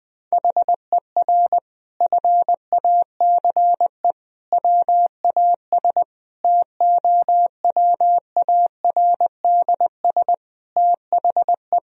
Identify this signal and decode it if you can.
Morse code, heard as a keyed tone, about 20 wpm, 700 Hz. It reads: HER FACE WAS TOWARDS THE
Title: Sentences from Top 1000 Words - Review Part 2 - 20wpm